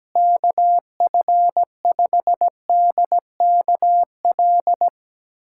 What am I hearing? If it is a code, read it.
Morse code, heard as a keyed tone, about 17 wpm, 700 Hz. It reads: KF5DKL
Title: USA Callsign DE Callsign - 6 Characters - 17wpm